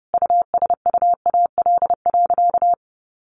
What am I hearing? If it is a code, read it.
Morse code, heard as a keyed tone, about 30 wpm, 700 Hz. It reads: USUAL.
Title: Sentences from Top 1400 Words 30wpm